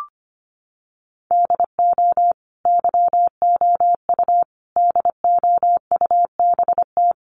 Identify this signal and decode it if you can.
Morse code, heard as a keyed tone, about 25 wpm, 700 Hz. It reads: DO YOU DOUBT
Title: Sentences from Top 1500 Words 25wpm